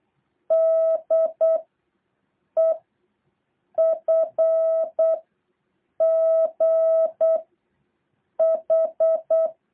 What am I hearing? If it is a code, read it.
Morse code, heard as a keyed tone, about 8 wpm, 650 Hz. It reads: D E F G H
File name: jh8cbh / 2023cw